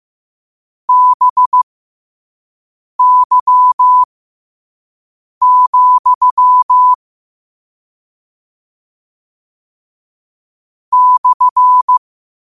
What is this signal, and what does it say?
Morse code, heard as a keyed tone, about 15 wpm, spaced as for 5 wpm, 1000 Hz. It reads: BY, /
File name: qotdfiles / 5wpm